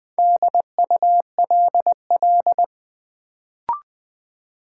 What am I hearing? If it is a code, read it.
Morse code, heard as a keyed tone, about 20 wpm, 700 Hz. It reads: DULL
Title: Build A Word - 2 Letters At A Time - 4-Letter Words - 20wpm